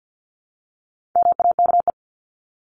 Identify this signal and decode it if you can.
Morse code, heard as a keyed tone, about 54 wpm, 700 Hz. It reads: MAKE
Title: Top 300 Words 50wpm